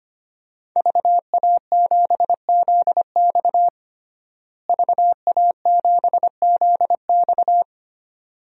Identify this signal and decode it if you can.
Morse code, heard as a keyed tone, about 25 wpm, 700 Hz. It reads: VA7ZX VA7ZX
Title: DXCC-100 Call Signs - 5 Characters - Repeated 2x - 25wpm